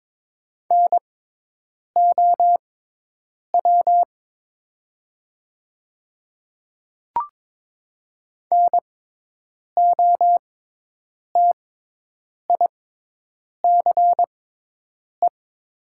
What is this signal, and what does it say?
Morse code, heard as a keyed tone, about 22 wpm, spaced as for 8 wpm, 700 Hz. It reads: NOW NOTICE
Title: 6x Character Spacing - Top 500 Words - 22wpm